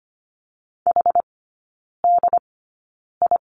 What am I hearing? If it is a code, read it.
Morse code, heard as a keyed tone, about 25 wpm, spaced as for 9 wpm, 700 Hz. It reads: HDI
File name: course / 25:10wpm / 093 Lesson 16e - Letter c - Three Characters 25wpm